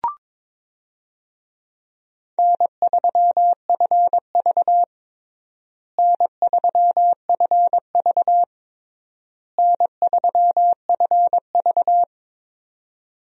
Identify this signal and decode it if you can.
Morse code, heard as a keyed tone, about 22 wpm, 700 Hz. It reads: N3FV N3FV N3FV
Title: USA Call Signs - 4 Characters - Repeated 3x - 22wpm